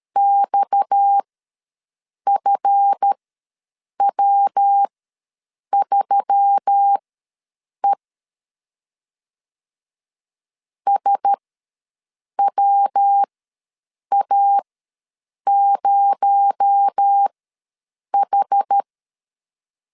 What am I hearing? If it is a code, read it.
Morse code, heard as a keyed tone, about 13 wpm, spaced as for 7 wpm, 800 Hz. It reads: XFW3E SWA0H
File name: LEZIONI / 9 286 ws 12wpm 3351